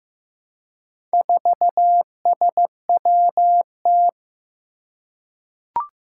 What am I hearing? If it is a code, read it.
Morse code, heard as a keyed tone, about 15 wpm, 700 Hz. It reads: 4SWT
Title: Four Challenging Letter-Number Combinations 15wpm